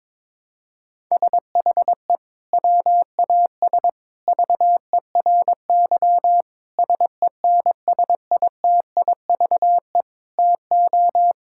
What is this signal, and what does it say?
Morse code, heard as a keyed tone, about 22 wpm, 700 Hz. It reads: SHE WAS VERY SENSITIVE TO
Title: Sentences from Top 1600 Words 22wpm